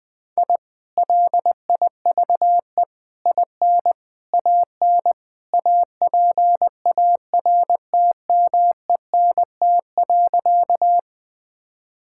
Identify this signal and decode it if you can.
Morse code, heard as a keyed tone, about 20 wpm, 700 Hz. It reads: I LIVE IN AN APARTMENT.
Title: Sentences from Top 1000 Words - Review Part 6 - 20wpm